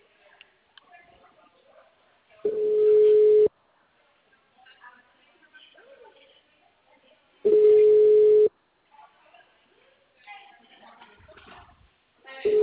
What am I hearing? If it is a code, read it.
Ringback tone (Europe, 425 Hz), with speech.
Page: Cordeiropolis 2